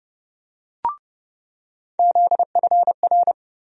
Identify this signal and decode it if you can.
Morse code, heard as a keyed tone, about 30 wpm, 700 Hz. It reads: ZFR
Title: USA Call Sign Postfixes - 3 Letters 30wpm